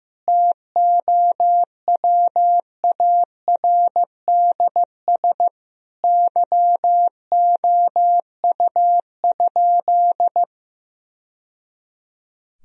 Morse code, heard as a keyed tone, about 15 wpm, 700 Hz. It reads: TOWARDS YOU?
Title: Sentences from Top 1000 Words - Review Part 2 - 15wpm